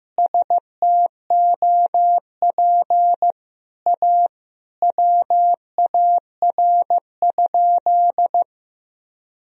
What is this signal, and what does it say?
Morse code, heard as a keyed tone, about 16 wpm, 700 Hz. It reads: STOP A WAR?
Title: Sentences from Top 1000 Words - Review Part 2 - 15wpm